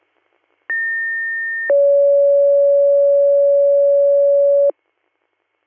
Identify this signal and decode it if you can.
Two-tone page, heard as a one-second tone, then a three-second tone, about 1800 then 560 Hz. It is heard from 0.7 to 4.7 s.